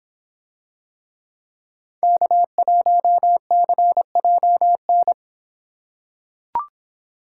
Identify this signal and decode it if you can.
Morse code, heard as a keyed tone, about 25 wpm, 700 Hz. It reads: K1CJN